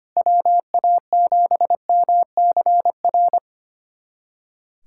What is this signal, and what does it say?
Morse code, heard as a keyed tone, about 25 wpm, 700 Hz. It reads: WA7MCR
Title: USA Callsign DE Callsign - 6 Characters - 25wpm